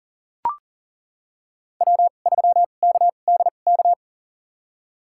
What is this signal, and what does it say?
Morse code, heard as a keyed tone, about 40 wpm, spaced as for 29 wpm, 700 Hz. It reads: W3KDK
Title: USA Call Signs - 5 Characters - 2x Character Spacing - 40wpm